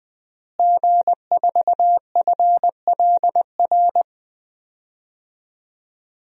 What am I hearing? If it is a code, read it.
Morse code, heard as a keyed tone, about 20 wpm, 700 Hz. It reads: G4FLR
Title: DXCC-100 Call Signs - 5 Characters - Repeated 3x - 20wpm